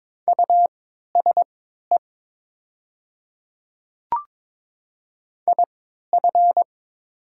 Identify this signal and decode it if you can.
Morse code, heard as a keyed tone, about 22 wpm, spaced as for 12 wpm, 700 Hz. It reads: USE IF
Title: 3x Character Spacing - Top 100 Words - 22wpm